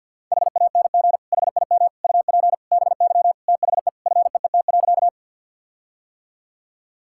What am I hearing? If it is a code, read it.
Morse code, heard as a keyed tone, about 50 wpm, 700 Hz. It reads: HANG HIM UP BY THE FEET.
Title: Sentences from Top 1600 Words 50wpm